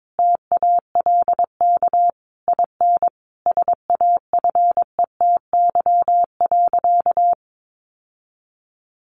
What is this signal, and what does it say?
Morse code, heard as a keyed tone, about 22 wpm, 700 Hz. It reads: TALK IN SAFETY.
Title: Sentences from Top 700 Words 22wpm